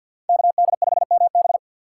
Morse code, heard as a keyed tone, about 49 wpm, 700 Hz. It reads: KD5MB
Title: USA Call Signs - 5 Characters - 3x Character Spacing - 50wpm